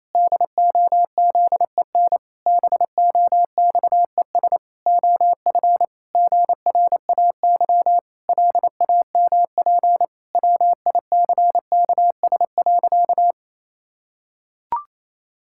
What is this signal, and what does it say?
Morse code, heard as a keyed tone, about 28 wpm, 700 Hz. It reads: DOZEN BOXES OF GRAY LAMP WICKS.
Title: Pangram Challenge v2 - 28wpm